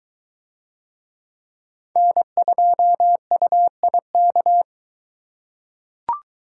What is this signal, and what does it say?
Morse code, heard as a keyed tone, about 22 wpm, 700 Hz. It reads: N2UIK